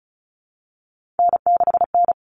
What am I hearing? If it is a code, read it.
Morse code, heard as a keyed tone, about 34 wpm, 700 Hz. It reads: N6N
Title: All Cut Numbers - 3-Digit 35wpm